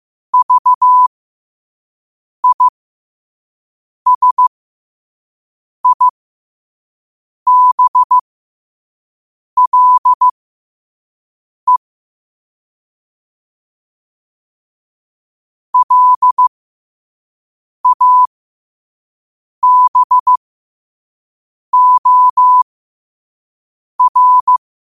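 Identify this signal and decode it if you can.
Morse code, heard as a keyed tone, about 15 wpm, spaced as for 5 wpm, 1000 Hz. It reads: VISIBLE LABOR
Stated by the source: Morse at 5 wpm.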